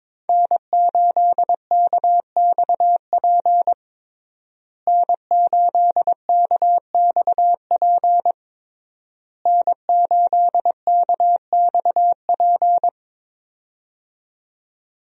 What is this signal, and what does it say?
Morse code, heard as a keyed tone, about 22 wpm, 700 Hz. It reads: N8KXP N8KXP N8KXP
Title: USA Call Signs - 5 Characters - Repeated 3x - 22wpm